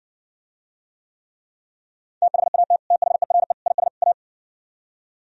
Morse code, heard as a keyed tone, about 59 wpm, 700 Hz. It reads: THAT THERE IS A